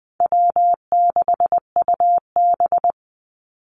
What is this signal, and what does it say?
Morse code, heard as a keyed tone, about 20 wpm, 700 Hz. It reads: W6UB
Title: NA SOTA Activator Callsigns 20wpm